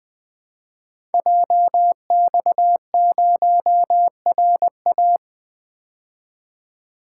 Morse code, heard as a keyed tone, about 20 wpm, 700 Hz. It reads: JX0RA
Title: Code Groups - 5 Letter:Numbers - 20wpm